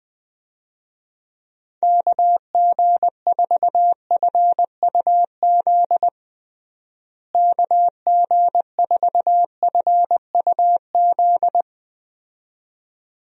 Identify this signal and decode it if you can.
Morse code, heard as a keyed tone, about 21 wpm, 700 Hz. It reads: KG4FUZ KG4FUZ
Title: USA Call Signs - 6 Characters - Repeated 2x - 20wpm